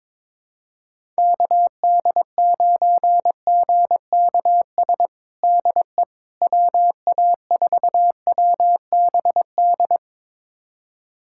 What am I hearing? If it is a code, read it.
Morse code, heard as a keyed tone, about 23 wpm, 700 Hz. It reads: KD9GKS DE WA4WBD
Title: USA Callsign DE Callsign - 6 Characters - 22wpm